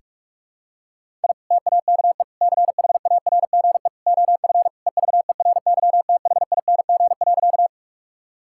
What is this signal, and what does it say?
Morse code, heard as a keyed tone, about 47 wpm, 700 Hz. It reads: I TAKE CHARGE OF EVERYTHING.
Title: Sentences from Top 1000 Words - Review Part 6 - 45wpm